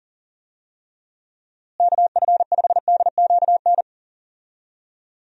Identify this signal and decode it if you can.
Morse code, heard as a keyed tone, about 37 wpm, 700 Hz. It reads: KF5DQN
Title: USA Call Signs - 6 Characters - Repeated 2x - 40wpm